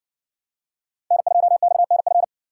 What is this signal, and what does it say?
Morse code, heard as a keyed tone, about 60 wpm, 700 Hz. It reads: N2XNF